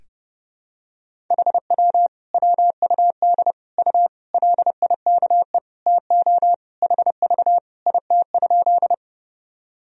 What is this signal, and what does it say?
Morse code, heard as a keyed tone, about 32 wpm, 700 Hz. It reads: HW WUD U LIKE TO HV IT?
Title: Sentences from Top 100 Words and Abbreviations 30wpm